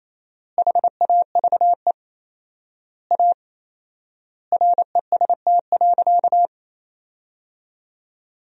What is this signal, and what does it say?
Morse code, heard as a keyed tone, about 28 wpm, 700 Hz. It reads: HAVE A REST.
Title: 4x Word-Spacing - Sentences from Top 500 Words - 28wpm